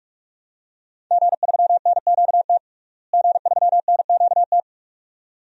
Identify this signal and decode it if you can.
Morse code, heard as a keyed tone, about 45 wpm, 700 Hz. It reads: G3NQT G3NQT
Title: DXCC-100 Call Signs - 5 Characters - Repeated 2x - 45wpm